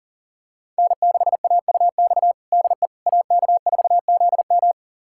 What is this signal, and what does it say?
Morse code, heard as a keyed tone, about 40 wpm, 700 Hz. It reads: N6AUX DE AK4ZM
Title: USA Callsign DE Callsign - 5 Characters - 40wpm